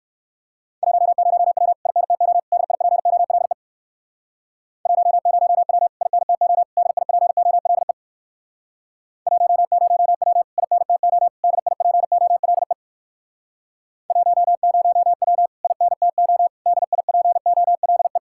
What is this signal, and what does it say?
Morse code, heard as a keyed tone, about 56 wpm, 700 Hz. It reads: 10W INTO DIPOLE 10W INTO DIPOLE 10W INTO DIPOLE 10W INTO DIPOLE